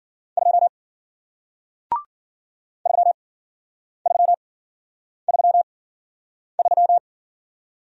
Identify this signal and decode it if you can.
Morse code, heard as a keyed tone, about 53 wpm, spaced as for 9 wpm, 700 Hz. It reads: 2 3333